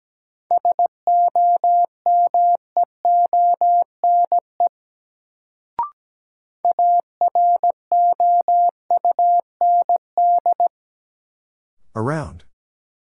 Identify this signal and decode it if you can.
Morse code, heard as a keyed tone, about 17 wpm, 700 Hz. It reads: SOMEONE AROUND
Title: Sentences from Top 600 Words - New Word Drill - 17wpm